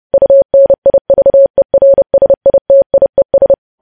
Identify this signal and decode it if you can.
Morse code, heard as a keyed tone, about 30 wpm, 550 Hz. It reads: UNIVERSITIES
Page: news headlines in morse code at 30 wpm